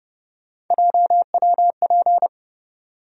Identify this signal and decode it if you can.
Morse code, heard as a keyed tone, about 30 wpm, 700 Hz. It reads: JWP
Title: USA Call Sign Postfixes - 3 Letters 30wpm